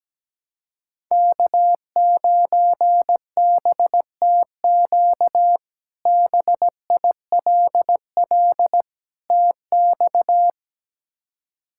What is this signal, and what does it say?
Morse code, heard as a keyed tone, about 17 wpm, 700 Hz. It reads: K9BTQ BILL TX
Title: K1USN SST QSO Practice - 17wpm